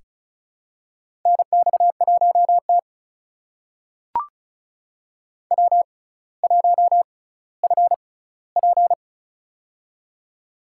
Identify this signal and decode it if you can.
Morse code, heard as a keyed tone, about 34 wpm, spaced as for 18 wpm, 700 Hz. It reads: NX1T W1FP
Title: USA Call Signs - 4 Characters - 6x Character Spacing - 35wpm